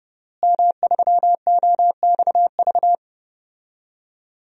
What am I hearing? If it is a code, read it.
Morse code, heard as a keyed tone, about 30 wpm, 700 Hz. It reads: M3OXV
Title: DXCC-100 Call Signs - 5 Characters - Repeated 3x - 30wpm